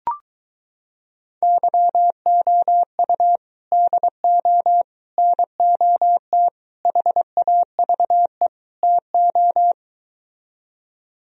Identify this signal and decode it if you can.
Morse code, heard as a keyed tone, about 22 wpm, 700 Hz. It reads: YOU DO NOT HAVE TO